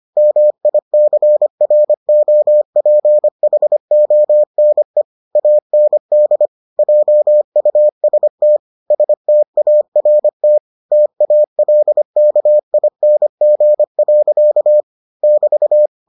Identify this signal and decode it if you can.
Morse code, heard as a keyed tone, about 25 wpm, 600 Hz. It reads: MICROPHONE AND JUST START TALKING. <BT>